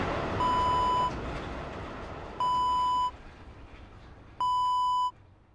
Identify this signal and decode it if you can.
There is no code.